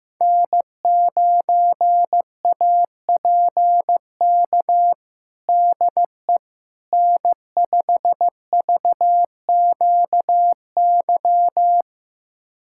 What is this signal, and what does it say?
Morse code, heard as a keyed tone, about 15 wpm, 700 Hz. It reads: N9APK DE N5VQY
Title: USA Callsign DE Callsign - 5 Characters - 15wpm